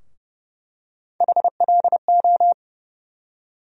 Morse code, heard as a keyed tone, about 30 wpm, 700 Hz. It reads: HLO
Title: USA Call Sign Postfixes - 3 Letters 30wpm